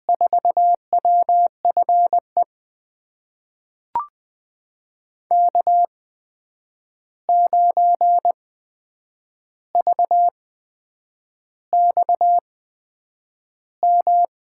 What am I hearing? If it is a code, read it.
Morse code, heard as a keyed tone, about 20 wpm, spaced as for 8 wpm, 700 Hz. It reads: 4WFE K9VXM
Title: USA Call Signs - 8x Character Spacing - 20wpm